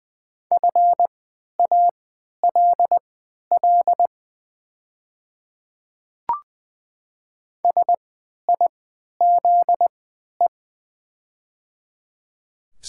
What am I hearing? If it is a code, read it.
Morse code, heard as a keyed tone, about 20 wpm, spaced as for 11 wpm, 690 Hz. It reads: FALL SIZE
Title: 3x Character Spacing - Top 500 Words - 20wpm